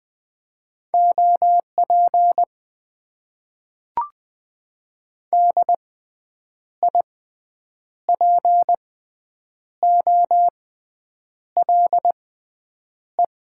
Morse code, heard as a keyed tone, about 20 wpm, spaced as for 8 wpm, 700 Hz. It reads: OP DIPOLE
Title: Top 90 QSO Elements - 6x Character Spacing - 20wpm